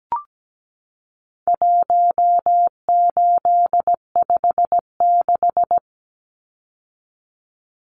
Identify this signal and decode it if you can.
Morse code, heard as a keyed tone, about 17 wpm, 700 Hz. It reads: 1856
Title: Numbers - 4-Digit 17wpm